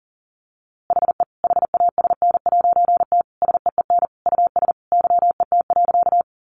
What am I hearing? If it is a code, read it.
Morse code, heard as a keyed tone, about 37 wpm, 700 Hz. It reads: HE HASN'T SEEN US YET.
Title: A Sound Of Thunder 40wpm